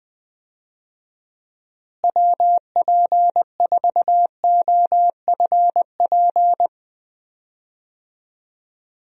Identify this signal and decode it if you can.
Morse code, heard as a keyed tone, about 21 wpm, 700 Hz. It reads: WP4OFP
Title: USA Call Signs - 6 Characters - Repeated 2x - 20wpm